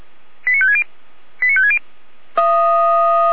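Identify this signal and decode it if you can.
Five-tone selective call (ZVEI), heard as quick runs of five tones, tones of about 70 ms. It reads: 89560 89560
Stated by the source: SDR-received ZVEI five-tone call 89167, from the audio server of the Orlamünde fire brigade (feuerwehr-orlamuende).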